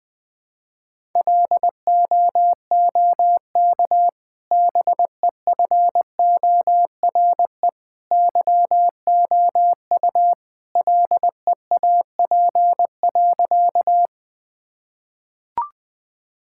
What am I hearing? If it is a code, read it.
Morse code, heard as a keyed tone, about 21 wpm, 700 Hz. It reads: LOOK BEFORE YOU LEAP.
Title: Easy Sentences with Hints - 20wpm